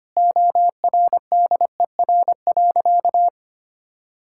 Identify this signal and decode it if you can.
Morse code, heard as a keyed tone, about 25 wpm, 700 Hz. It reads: ORDER.
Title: Sentences from Top 1600 Words 25wpm